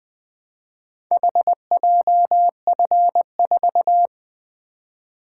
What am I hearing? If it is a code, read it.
Morse code, heard as a keyed tone, about 20 wpm, 700 Hz. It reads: HJF4
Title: Four Letter-Number Combinations - 20wpm